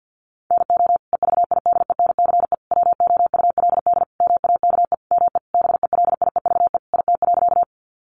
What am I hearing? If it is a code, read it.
Morse code, heard as a keyed tone, about 50 wpm, 700 Hz. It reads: NO EVIDENCE WOULD MAKE ME BELIEVE IT.
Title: Sentences from Top 1300 Words 50wpm